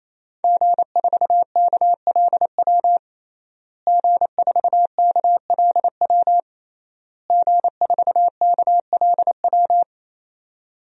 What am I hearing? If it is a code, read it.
Morse code, heard as a keyed tone, about 27 wpm, 700 Hz. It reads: G4KLW G4KLW G4KLW
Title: DXCC-100 Call Signs - 5 Characters - Repeated 3x - 28wpm